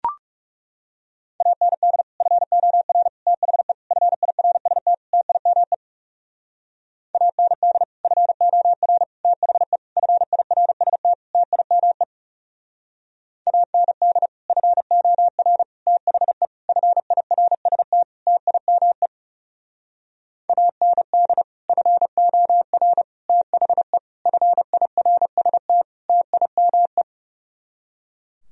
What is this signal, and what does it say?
Morse code, heard as a keyed tone, about 37 wpm, 700 Hz. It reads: AND FOR THE FIRST TIME AND FOR THE FIRST TIME AND FOR THE FIRST TIME AND FOR THE FIRST TIME
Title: Speed-Racing - Sets of 5 Words 30wpm